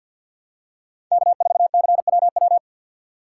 Morse code, heard as a keyed tone, about 53 wpm, 700 Hz. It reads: K4CWW
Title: USA Call Signs - 5 Characters - 6x Character Spacing - 50wpm